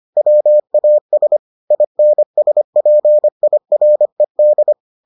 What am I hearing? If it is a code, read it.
Morse code, heard as a keyed tone, about 25 wpm, 600 Hz. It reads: WAS INSPIRED